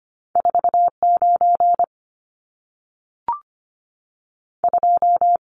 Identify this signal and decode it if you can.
Morse code, heard as a keyed tone, about 25 wpm, 700 Hz. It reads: 49 2
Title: Numbers - 2-Digit 25wpm